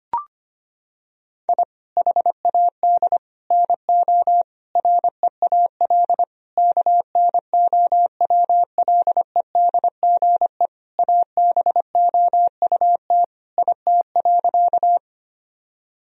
Morse code, heard as a keyed tone, about 25 wpm, 700 Hz. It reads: I HAD NO REAL KNOWLEDGE ABOUT IT.
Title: Sentences from Top 1000 Words - Review Part 3 - 25wpm